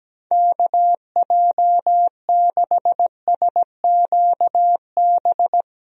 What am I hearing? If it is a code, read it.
Morse code, heard as a keyed tone, about 17 wpm, 700 Hz. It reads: KJ6SQB
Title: USA Call Signs - 6 Characters - 17wpm